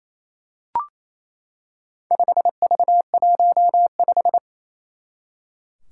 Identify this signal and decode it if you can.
Morse code, heard as a keyed tone, about 28 wpm, 700 Hz. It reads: 5V15
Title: Four Challenging Letter-Number Combinations 28wpm